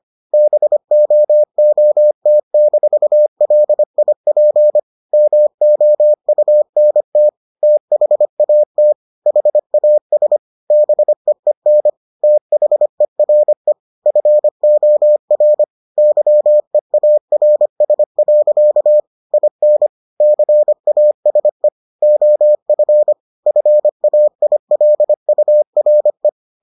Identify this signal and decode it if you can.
Morse code, heard as a keyed tone, about 25 wpm, 600 Hz. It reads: BOOT-LIP MOUNT THAT HAS BEEN THERE FOR YEARS. IN CASE OF FAILURE